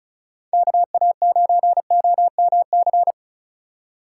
Morse code, heard as a keyed tone, about 36 wpm, 700 Hz. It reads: KA9OMC
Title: Speed Racing - USA Call Signs 35wpm